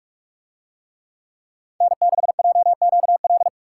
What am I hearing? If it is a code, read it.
Morse code, heard as a keyed tone, about 44 wpm, 700 Hz. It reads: N6JQL